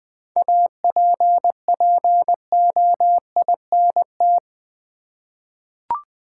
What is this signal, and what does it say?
Morse code, heard as a keyed tone, about 20 wpm, 700 Hz. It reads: APPOINT
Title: 7 Letter Words - First 3 Letters Given - 20wpm